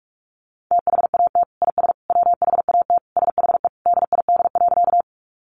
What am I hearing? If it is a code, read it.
Morse code, heard as a keyed tone, about 44 wpm, 700 Hz. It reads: THAT IS WHAT SHE DID.
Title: Sentences from Top 100 Words 45wpm